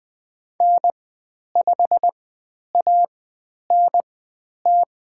Morse code, heard as a keyed tone, about 20 wpm, spaced as for 10 wpm, 700 Hz. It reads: N5ANT
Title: 102 Lesson 17g - Letter u - Call Signs 20wpm